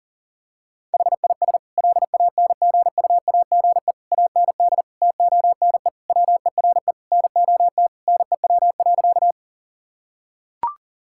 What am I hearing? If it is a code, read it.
Morse code, heard as a keyed tone, about 40 wpm, 700 Hz. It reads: HIS LANGUAGE AND TONE WERE NOT NEW.
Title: Sentences from Top 1600 Words 40wpm